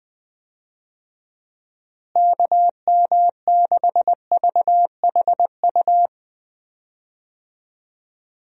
Morse code, heard as a keyed tone, about 21 wpm, 700 Hz. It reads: KM6VHU
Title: USA Call Signs - 6 Characters - Repeated 3x - 20wpm